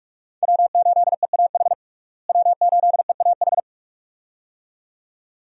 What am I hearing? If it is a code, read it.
Morse code, heard as a keyed tone, about 46 wpm, 700 Hz. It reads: W8EAH W8EAH
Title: USA Call Signs - 5 Characters - Repeated 2x - 45wpm